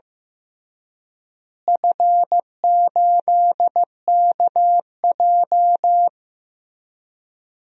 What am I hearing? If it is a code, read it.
Morse code, heard as a keyed tone, about 15 wpm, 700 Hz. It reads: F8KJ
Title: DXCC-100 Call Signs - 4 Characters - Repeated 3x - 15wpm